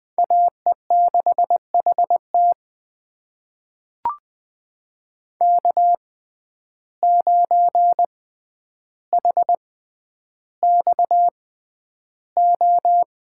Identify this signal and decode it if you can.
Morse code, heard as a keyed tone, about 20 wpm, spaced as for 10 wpm, 700 Hz. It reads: AE6HT K9HXO
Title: USA Call Signs - 6x Character Spacing - 20wpm